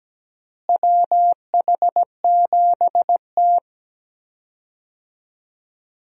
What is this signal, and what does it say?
Morse code, heard as a keyed tone, about 17 wpm, 700 Hz. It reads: WH7T
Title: USA Call Signs - 4 Characters - Repeated 2x - 17wpm